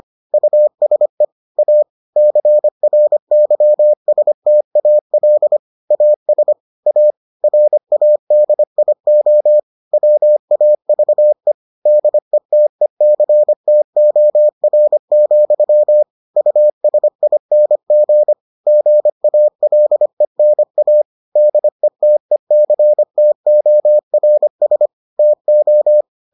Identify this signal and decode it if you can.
Morse code, heard as a keyed tone, about 25 wpm, 600 Hz. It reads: USE A CRYSTAL AS A RADIO WAVE DETECTOR, USING GALENA DETECTORS TO